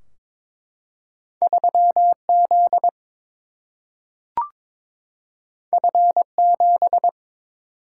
Morse code, heard as a keyed tone, about 22 wpm, 700 Hz. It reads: 3 Z F 7
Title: Single-Letter Single-Number 22wpm